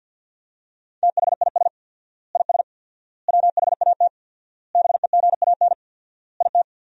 Morse code, heard as a keyed tone, about 47 wpm, 700 Hz. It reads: THIS IS WHAT BEGAN IT
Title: 4x Word-Spacing - Sentences from Top 300 Words - 50wpm